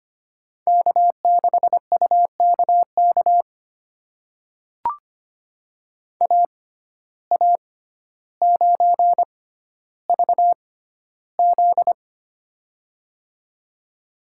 Morse code, heard as a keyed tone, about 25 wpm, spaced as for 13 wpm, 700 Hz. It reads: K6UKK AA9VZ
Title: USA Call Signs - 5 Characters - 6x Character Spacing - 25wpm